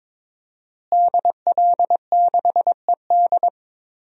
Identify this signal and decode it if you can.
Morse code, heard as a keyed tone, about 22 wpm, 700 Hz. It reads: DL6ED
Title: DXCC-100 Call Signs - 5 Characters - Repeated 2x - 22wpm